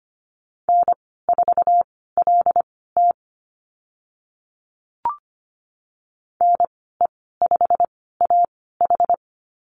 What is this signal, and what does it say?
Morse code, heard as a keyed tone, about 25 wpm, spaced as for 16 wpm, 690 Hz. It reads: N4LT NE5AH
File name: Lesson 15g - Number 5 - Call Signs 25wpm